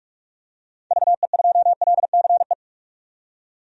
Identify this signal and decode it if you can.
Morse code, heard as a keyed tone, about 45 wpm, 700 Hz. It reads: VE2LCE
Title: DXCC-100 Call Signs - 6 Characters - 45wpm